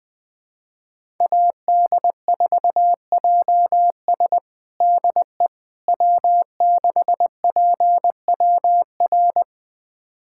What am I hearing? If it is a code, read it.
Morse code, heard as a keyed tone, about 20 wpm, 700 Hz. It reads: AD4JS DE W6PWR
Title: USA Callsign DE Callsign - 5 Characters - 20wpm